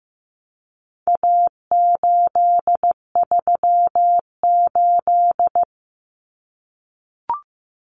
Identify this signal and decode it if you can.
Morse code, heard as a keyed tone, about 15 wpm, 700 Hz. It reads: A838
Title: Common Cut Numbers - 4-Digit 15wpm